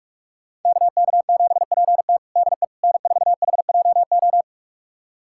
Morse code, heard as a keyed tone, about 48 wpm, 700 Hz. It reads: KK7PT DE N4HJO
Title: USA Callsign DE Callsign - 5 Characters - 45wpm